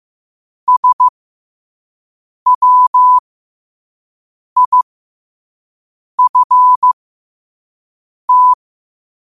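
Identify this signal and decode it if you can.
Morse code, heard as a keyed tone, about 15 wpm, spaced as for 5 wpm, 1000 Hz. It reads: SWIFT
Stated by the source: Morse code at 5 wpm.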